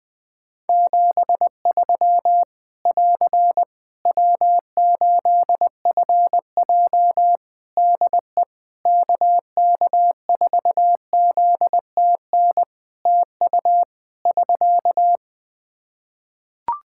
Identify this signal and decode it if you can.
Morse code, heard as a keyed tone, about 21 wpm, 700 Hz. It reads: 73 <AR> W8FJ DE KK4ZTN TU <SK>
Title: Practice QSO - Exchange 4 - Ending - 20wpm